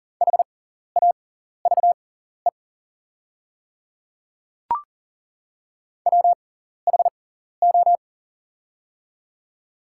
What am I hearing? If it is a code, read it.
Morse code, heard as a keyed tone, about 40 wpm, spaced as for 14 wpm, 700 Hz. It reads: HAVE WHO
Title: 6x Character Spacing - Top 100 Words - 40wpm